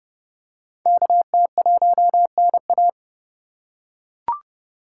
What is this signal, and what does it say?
Morse code, heard as a keyed tone, about 31 wpm, 700 Hz. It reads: KT1NA